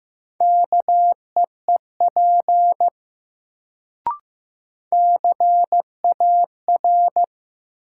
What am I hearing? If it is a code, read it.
Morse code, heard as a keyed tone, about 15 wpm, 700 Hz. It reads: KEEP CAR
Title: Sentences from Top 300 Words - New Word Drill - 15wpm